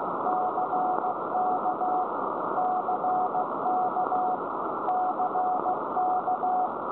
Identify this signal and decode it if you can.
Morse code, heard as a keyed tone, about 15 wpm, 740 Hz, with noise. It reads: CK CK CK
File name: storage / websdr G8HXT Wheathampstead 1347km 14149.0kHz